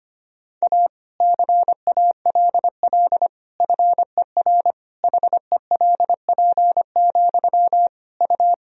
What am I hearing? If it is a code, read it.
Morse code, heard as a keyed tone, about 25 wpm, 700 Hz. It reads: A CALL FER HELP, U
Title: Sentences from Top 300 Words and Abbreviations 25wpm